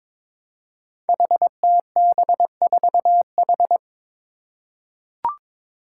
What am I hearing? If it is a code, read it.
Morse code, heard as a keyed tone, about 22 wpm, 700 Hz. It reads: HTB4H
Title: Five Challenging Letter-Number Combinations 22wpm